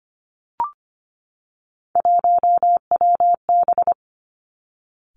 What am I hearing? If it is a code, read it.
Morse code, heard as a keyed tone, about 25 wpm, 700 Hz. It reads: 1WB
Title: Three Challenging Letter-Number Combinations 25wpm